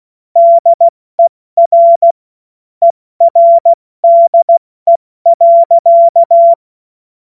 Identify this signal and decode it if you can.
Morse code, heard as a keyed tone, about 16 wpm, spaced as for 14 wpm, 680 Hz. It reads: DER ERDE.
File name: Morsekurs / Witze / Witz 16. August 70 F80